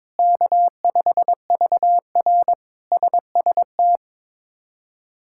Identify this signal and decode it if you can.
Morse code, heard as a keyed tone, about 22 wpm, 700 Hz. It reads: K5VR SST
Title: K1USN SST QSO Practice - 22wpm